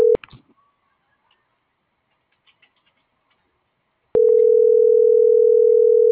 Ringback tone (North America), 440 + 480 Hz.